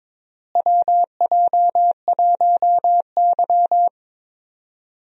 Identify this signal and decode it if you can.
Morse code, heard as a keyed tone, about 21 wpm, 700 Hz. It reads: WJ1Y
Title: USA Call Signs - 4 Characters - Repeated 3x - 22wpm